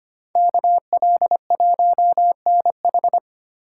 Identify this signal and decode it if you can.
Morse code, heard as a keyed tone, about 27 wpm, 700 Hz. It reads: KL1NH
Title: Speed Racing - USA 2x2 Call Signs 25wpm